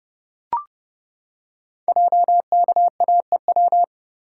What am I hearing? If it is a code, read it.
Morse code, heard as a keyed tone, about 30 wpm, 700 Hz. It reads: JKAEW
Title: Code Groups - 5 Letters - 30wpm